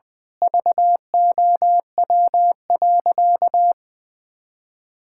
Morse code, heard as a keyed tone, about 20 wpm, 700 Hz. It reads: VOW.
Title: Pangram Challenge v2 - 20wpm